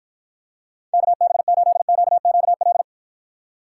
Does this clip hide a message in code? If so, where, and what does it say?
Yes: Morse code from 0.9 to 2.8 s, heard as a keyed tone, about 52 wpm, 700 Hz. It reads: KB8QQL